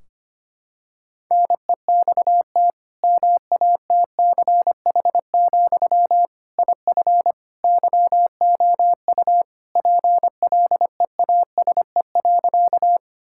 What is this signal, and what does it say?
Morse code, heard as a keyed tone, about 25 wpm, 700 Hz. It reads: NEXT MATCH, IF YOU PLEASE.
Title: Sentences from Top 1600 Words 25wpm